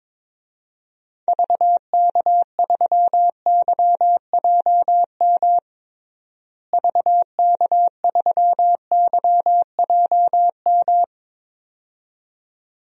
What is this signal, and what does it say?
Morse code, heard as a keyed tone, about 22 wpm, 700 Hz. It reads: VK3YJM VK3YJM
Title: DXCC-100 Call Signs - 6 Characters - Repeated 2x - 22wpm